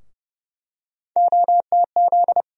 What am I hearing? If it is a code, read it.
Morse code, heard as a keyed tone, about 31 wpm, 700 Hz. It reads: OTZ
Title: USA Call Sign Postfixes - 3 Letters 30wpm